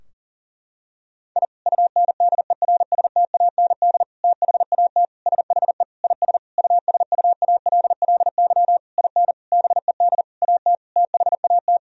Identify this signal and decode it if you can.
Morse code, heard as a keyed tone, about 40 wpm, 700 Hz. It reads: I UNDERSTAND THAT SHE IS USUALLY IN BED AT THAT
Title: Sentences from Top 1000 Words - Review Part 3 - 40wpm